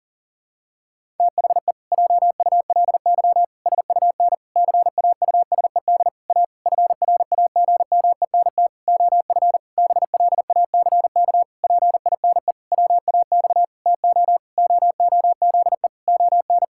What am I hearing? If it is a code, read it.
Morse code, heard as a keyed tone, about 40 wpm, 700 Hz. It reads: THE JULY SUN CAUSED A FRAGMENT OF BLACK PINE WAX TO OOZE ON
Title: Pangram Challenge v2 - 40wpm